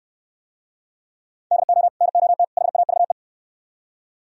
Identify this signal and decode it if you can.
Morse code, heard as a keyed tone, about 64 wpm, 700 Hz. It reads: DO NOT HAVE